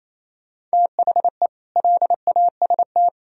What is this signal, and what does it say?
Morse code, heard as a keyed tone, about 38 wpm, 700 Hz. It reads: THE LAST